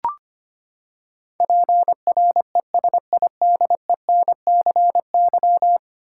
Morse code, heard as a keyed tone, about 25 wpm, 700 Hz. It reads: PRESIDENCY
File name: Ten Letter Words - 25wpm